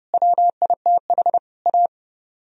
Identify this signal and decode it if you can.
Morse code, heard as a keyed tone, about 40 wpm, 700 Hz. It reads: WITH A